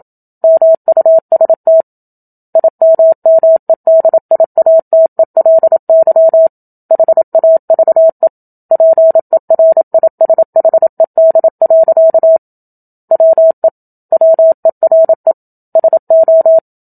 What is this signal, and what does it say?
Morse code, heard as a keyed tone, about 27 wpm, 650 Hz. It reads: MUST IMMEDIATELY HAVE PERISHED. WE WERE SO
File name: Book Two / 01-Chapter 1 - 27 WPM